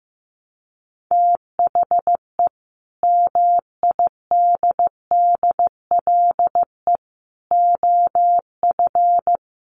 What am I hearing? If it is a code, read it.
Morse code, heard as a keyed tone, about 15 wpm, 700 Hz. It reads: THE MIDDLE OF
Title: Sets of 5 Words Encore2 15wpm